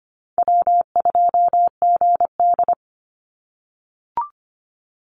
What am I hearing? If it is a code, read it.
Morse code, heard as a keyed tone, about 25 wpm, 700 Hz. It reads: W2GD
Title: Easy Call Signs - 25wpm